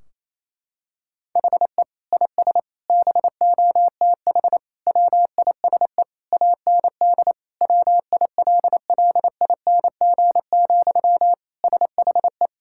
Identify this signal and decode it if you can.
Morse code, heard as a keyed tone, about 28 wpm, 700 Hz. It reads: HE IS BOTH WISE AND WILLING, SHE
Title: Sentences from Top 1600 Words 28wpm